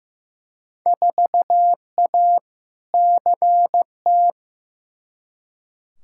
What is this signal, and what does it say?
Morse code, heard as a keyed tone, about 15 wpm, 700 Hz. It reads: 4A CT
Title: Field Day Exchanges Encore - 15wpm